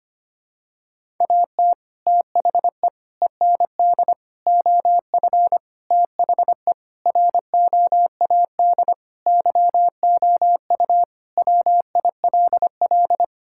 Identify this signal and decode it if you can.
Morse code, heard as a keyed tone, about 25 wpm, 700 Hz. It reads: AT THE END OF THE ROAD YOU WILL
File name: practice / 25wpm / Sentences with 3-Word Phrases - 25wpm